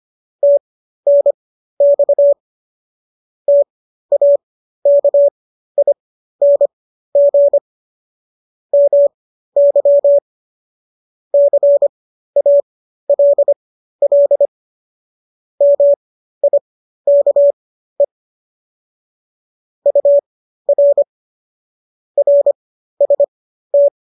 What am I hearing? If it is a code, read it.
Morse code, heard as a keyed tone, about 25 wpm, spaced as for 13 wpm, 570 Hz. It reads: TNX TAKING MY CALL MIKE UR RST